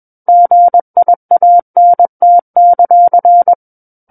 Morse code, heard as a keyed tone, about 21 wpm, 700 Hz. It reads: GIANT;
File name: Book One / 12-Chapter 12 - 21 WPM